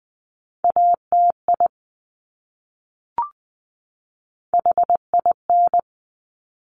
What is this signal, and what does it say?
Morse code, heard as a keyed tone, about 20 wpm, 700 Hz. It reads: ATI HIN
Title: Top Three Letter Combinations in Words 20wpm